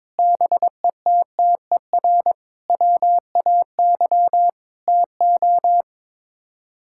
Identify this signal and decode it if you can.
Morse code, heard as a keyed tone, about 22 wpm, 700 Hz. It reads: BETTER WAY TO
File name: practice / 22wpm / Sets of 8 Words - 22wpm